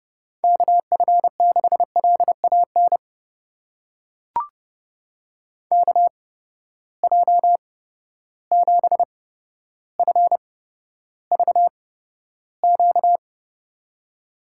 Morse code, heard as a keyed tone, about 30 wpm, spaced as for 13 wpm, 700 Hz. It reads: KF6LAN KJ7FVQ
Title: USA Call Signs - 6 Characters - 8x Character Spacing - 30wpm